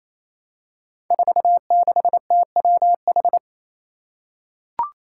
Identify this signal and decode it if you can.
Morse code, heard as a keyed tone, about 28 wpm, 700 Hz. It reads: 46TWH
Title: Five Challenging Letter-Number Combinations 28wpm